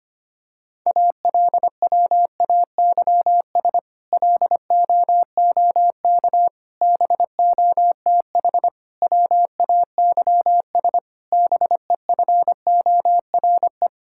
Morse code, heard as a keyed tone, about 25 wpm, 700 Hz. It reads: ALWAYS LOOK BOTH WAYS BEFORE
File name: practice / 25wpm / Difficult Sentences with Hints - 25wpm